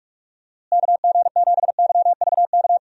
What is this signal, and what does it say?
Morse code, heard as a keyed tone, about 46 wpm, 700 Hz. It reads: KG7YVK
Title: USA Call Signs - 6 Characters - Repeated 2x - 45wpm